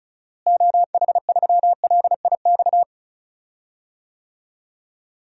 Morse code, heard as a keyed tone, about 36 wpm, 700 Hz. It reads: OH3LIX
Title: DXCC-100 Call Signs - 6 Characters - Repeated 2x - 35wpm